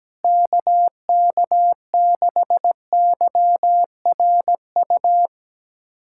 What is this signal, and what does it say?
Morse code, heard as a keyed tone, about 17 wpm, 700 Hz. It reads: KK6YRU
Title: USA Callsign DE Callsign - 6 Characters - 17wpm